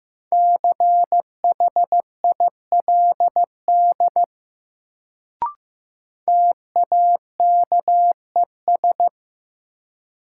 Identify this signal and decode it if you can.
Morse code, heard as a keyed tone, about 15 wpm, 700 Hz. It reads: CHILD TAKES
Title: Sets of 5 Words Encore - New Word Drill - 15wpm